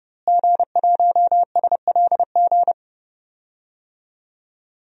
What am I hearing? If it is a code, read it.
Morse code, heard as a keyed tone, about 32 wpm, 700 Hz. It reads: G1SLG
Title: DXCC-100 Call Signs - 5 Characters - Repeated 2x - 30wpm